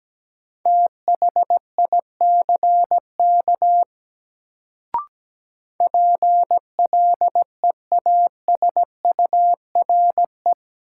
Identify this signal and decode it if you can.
Morse code, heard as a keyed tone, about 17 wpm, 700 Hz. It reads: THICK PLEASURE
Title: Sentences from Top 1600 Words - New Word Drill Alt - 17wpm